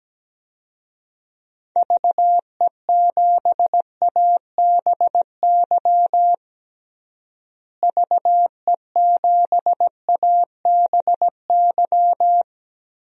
Morse code, heard as a keyed tone, about 17 wpm, 700 Hz. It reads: VE7ABY VE7ABY
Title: DXCC-100 Call Signs - 6 Characters - Repeated 2x - 17wpm